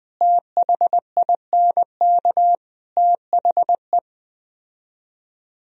Morse code, heard as a keyed tone, about 20 wpm, 700 Hz. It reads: THINK THE
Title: Speed-Racing - Sets of 5 Words 17wpm